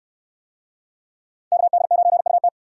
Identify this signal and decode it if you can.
Morse code, heard as a keyed tone, about 69 wpm, 700 Hz. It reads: KD9UT